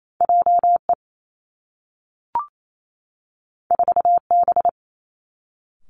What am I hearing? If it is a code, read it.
Morse code, heard as a keyed tone, about 28 wpm, 700 Hz. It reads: J E 4 B
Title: Two Challenging Letter-Number Combinations 28wpm